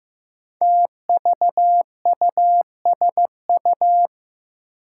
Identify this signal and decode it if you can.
Morse code, heard as a keyed tone, about 15 wpm, 700 Hz. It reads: TVUSU
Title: Five Challenging Letter-Number Combinations 15wpm